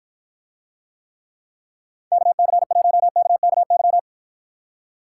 Morse code, heard as a keyed tone, about 53 wpm, 700 Hz. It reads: KC1KKY